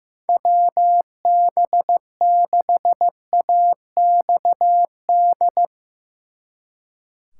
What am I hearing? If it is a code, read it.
Morse code, heard as a keyed tone, about 15 wpm, 700 Hz. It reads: WB6AXD
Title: Speed Racing - USA Call Signs 15wpm